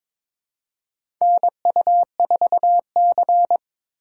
Morse code, heard as a keyed tone, about 22 wpm, 700 Hz. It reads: NU4C
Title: USA Call Signs - 4 Characters - 4x Character Spacing - 22wpm